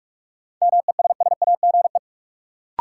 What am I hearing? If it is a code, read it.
Morse code, heard as a keyed tone, about 47 wpm, 700 Hz. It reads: MESSAGE
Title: Sentences from Top 1100 Words - New Word Drill - 45wpm